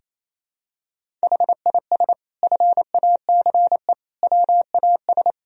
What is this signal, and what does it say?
Morse code, heard as a keyed tone, about 28 wpm, 700 Hz. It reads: HIS FACE WAS
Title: Sentences from Top 1000 Words - Review Part 4 - 28wpm